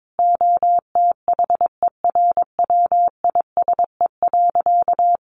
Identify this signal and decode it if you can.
Morse code, heard as a keyed tone, about 22 wpm, 700 Hz. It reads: OTHERWISE.
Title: Sentences from Top 1200 Words 22wpm